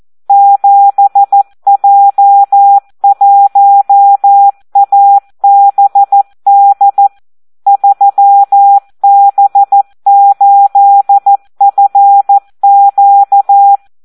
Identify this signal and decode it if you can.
Morse code, heard as a keyed tone, about 14 wpm, 800 Hz. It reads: 7J1ABD 3B8FQ
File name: Audio practica CW / SD 1 14